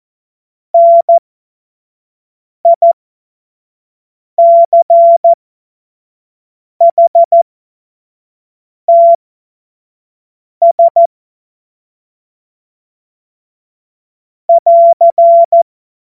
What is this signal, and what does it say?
Morse code, heard as a keyed tone, about 14 wpm, spaced as for 5 wpm, 680 Hz. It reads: NICHTS <AR>
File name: Morsekurs / Witze / Witz 21. Dezember 2021, 50 F70